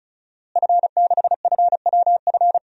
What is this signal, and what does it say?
Morse code, heard as a keyed tone, about 35 wpm, 700 Hz. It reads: F6FWF
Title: DXCC-100 Call Signs - 5 Characters - Repeated 2x - 35wpm